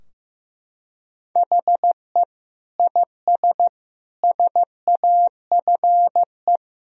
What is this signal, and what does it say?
Morse code, heard as a keyed tone, about 15 wpm, 700 Hz. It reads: HE IS SAFE
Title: Sentences from Top 1000 Words - Review Part 4 - 15wpm